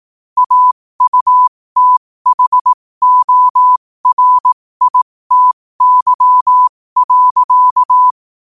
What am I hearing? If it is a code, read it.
Morse code, heard as a keyed tone, about 18 wpm, spaced as for 15 wpm, 1000 Hz. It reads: AUTHORITY.
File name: 15wpm